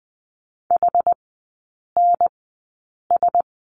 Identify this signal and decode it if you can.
Morse code, heard as a keyed tone, about 20 wpm, spaced as for 8 wpm, 690 Hz. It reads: HNS
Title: 081 Lesson 14e - Number 2 - Three Characters 20wpm